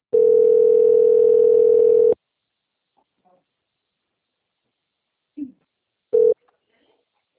Ringback tone (North America), 440 + 480 Hz.
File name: out- - - - - .5182